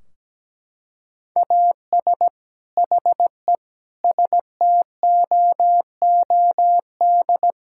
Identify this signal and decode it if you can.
Morse code, heard as a keyed tone, about 17 wpm, 700 Hz. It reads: AS HE STOOD
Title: Sentences from Top 1600 Words 17wpm